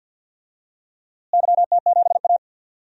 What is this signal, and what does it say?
Morse code, heard as a keyed tone, about 52 wpm, 700 Hz. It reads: YT7A
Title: DXCC-100 Call Signs - 4 Characters - Repeated 3x - 50wpm